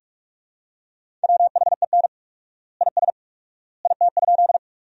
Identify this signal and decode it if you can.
Morse code, heard as a keyed tone, about 46 wpm, 700 Hz. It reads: WHEN IS IT?
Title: 4x Word-Spacing - Sentences from Top 100 Words - 45wpm